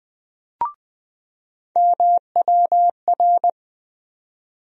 Morse code, heard as a keyed tone, about 20 wpm, 700 Hz. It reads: MWR
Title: USA Call Sign Postfixes - 3 Letters 20wpm